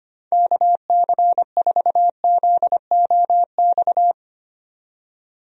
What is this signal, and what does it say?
Morse code, heard as a keyed tone, about 25 wpm, 700 Hz. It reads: KC4ZOX
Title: USA Call Signs - 6 Characters - 25wpm